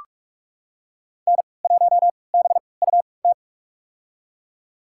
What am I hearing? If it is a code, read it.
Morse code, heard as a keyed tone, about 41 wpm, spaced as for 25 wpm, 700 Hz. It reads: N1BUT
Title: USA Call Signs - 3x Character Spacing - 45wpm